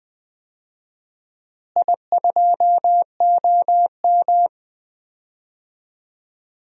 Morse code, heard as a keyed tone, about 21 wpm, 700 Hz. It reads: I2OM
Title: DXCC-100 Call Signs - 4 Characters - Repeated 3x - 20wpm